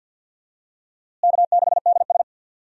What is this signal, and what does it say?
Morse code, heard as a keyed tone, about 52 wpm, 700 Hz. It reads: K6DS
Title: USA Call Signs - 4 Characters - Repeated 2x - 50wpm